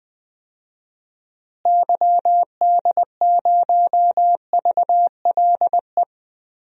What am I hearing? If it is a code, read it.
Morse code, heard as a keyed tone, about 20 wpm, 700 Hz. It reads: YD0VLE
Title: DXCC-100 Call Signs - 6 Characters - Repeated 3x - 20wpm